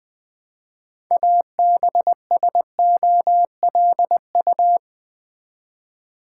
Morse code, heard as a keyed tone, about 20 wpm, 700 Hz. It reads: ABSOLU
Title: Build A Word - 3 Letters At A Time - 8-Letter Words - 20wpm